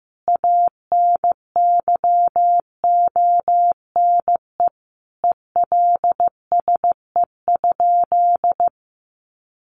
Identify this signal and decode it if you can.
Morse code, heard as a keyed tone, about 15 wpm, 700 Hz. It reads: ANYONE ELSE?
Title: Sentences from Top 900 Words 15wpm